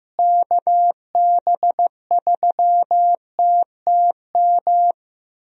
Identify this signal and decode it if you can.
Morse code, heard as a keyed tone, about 15 wpm, 700 Hz. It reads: KB3TTM
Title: USA Call Signs - 6 Characters - Repeated 3x - 15wpm